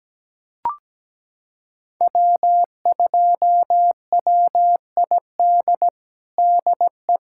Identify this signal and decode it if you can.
Morse code, heard as a keyed tone, about 17 wpm, 700 Hz. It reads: W2WID DE
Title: USA Callsign DE Callsign - 5 Characters - 17wpm